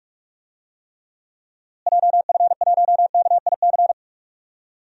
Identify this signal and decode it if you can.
Morse code, heard as a keyed tone, about 41 wpm, 700 Hz. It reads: JF1KIC
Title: DXCC-100 Call Signs - 6 Characters - Repeated 3x - 45wpm